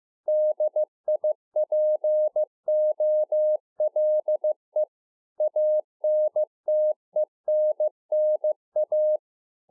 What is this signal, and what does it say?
Morse code, heard as a keyed tone, about 15 wpm, 610 Hz. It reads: DIPOLE ANTENNA